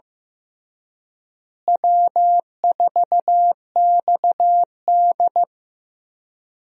Morse code, heard as a keyed tone, about 15 wpm, 700 Hz. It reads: W4XD
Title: USA Call Signs - 4 Characters - Repeated 3x - 15wpm